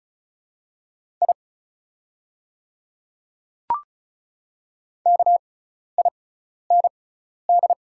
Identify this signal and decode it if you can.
Morse code, heard as a keyed tone, about 36 wpm, spaced as for 12 wpm, 690 Hz. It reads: I KIND
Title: 6x Character Spacing - Top 200 Words - 35wpm